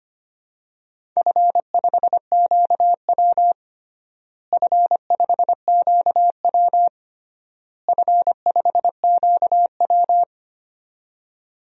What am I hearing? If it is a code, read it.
Morse code, heard as a keyed tone, about 25 wpm, 700 Hz. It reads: F5QW F5QW F5QW
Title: DXCC-100 Call Signs - 4 Characters - Repeated 3x - 25wpm